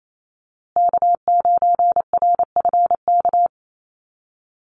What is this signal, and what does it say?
Morse code, heard as a keyed tone, about 28 wpm, 700 Hz. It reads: K9RFK
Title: USA Call Signs Repeated 3x 28wpm